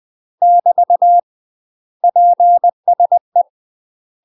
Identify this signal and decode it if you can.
Morse code, heard as a keyed tone, about 20 wpm, 700 Hz. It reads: <BT> PSE
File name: UK MASTER QSOs TRAINING / 20WPM / UK MASTER QSOs 0060